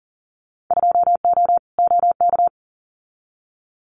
Morse code, heard as a keyed tone, about 38 wpm, 700 Hz. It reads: 2O OK
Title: Winter Field Day Exchanges 40wpm